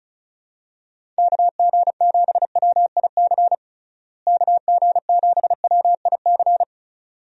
Morse code, heard as a keyed tone, about 36 wpm, 700 Hz. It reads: KG7WIC KG7WIC
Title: USA Call Signs - 6 Characters - Repeated 2x - 35wpm